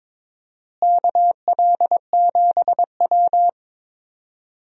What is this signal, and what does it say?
Morse code, heard as a keyed tone, about 22 wpm, 700 Hz. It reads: KL7W
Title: DXCC-100 Call Signs - 4 Characters - Repeated 3x - 22wpm